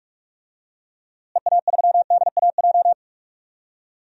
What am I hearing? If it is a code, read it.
Morse code, heard as a keyed tone, about 46 wpm, 700 Hz. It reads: EA3DAJ
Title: DXCC-100 Call Signs - 6 Characters - Repeated 3x - 45wpm